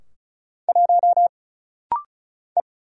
Morse code, heard as a keyed tone, about 34 wpm, 700 Hz. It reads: 1 E
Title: Single Letter-Number Alt - Rapid-Fire 35wpm